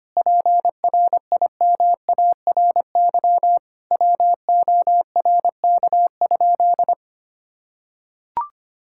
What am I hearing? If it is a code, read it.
Morse code, heard as a keyed tone, about 25 wpm, 700 Hz. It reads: PRIMARY WORK?
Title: Sentences from Top 1000 Words - Review Part 5 - 25wpm